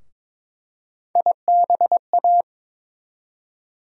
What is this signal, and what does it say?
Morse code, heard as a keyed tone, about 22 wpm, 700 Hz. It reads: IBA
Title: USA Call Sign Postfixes - 3 Letters 22wpm